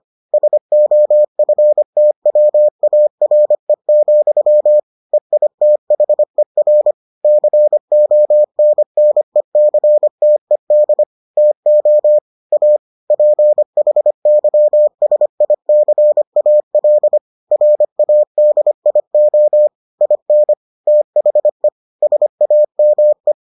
Morse code, heard as a keyed tone, about 25 wpm, 600 Hz. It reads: SOFTWARE, EITHER CONNECTED TO A PHYSICAL RADIO IN THE SAME